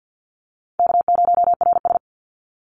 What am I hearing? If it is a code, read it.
Morse code, heard as a keyed tone, about 50 wpm, 700 Hz. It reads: K0RS
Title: NA SOTA Chaser Callsigns 50wpm